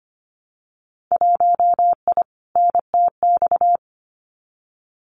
Morse code, heard as a keyed tone, about 25 wpm, 700 Hz. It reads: 1I NTX
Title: Winter Field Day Exchanges Encore 25wpm